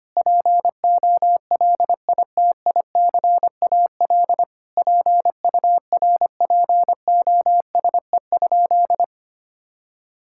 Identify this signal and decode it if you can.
Morse code, heard as a keyed tone, about 25 wpm, 700 Hz. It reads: POLITICAL PURPOSE?
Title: Sentences from Top 1000 Words - Review Part 5 - 25wpm